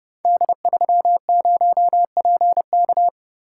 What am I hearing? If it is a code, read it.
Morse code, heard as a keyed tone, about 30 wpm, 700 Hz. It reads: D30PK
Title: Code Groups - 5 Letter:Numbers - 30wpm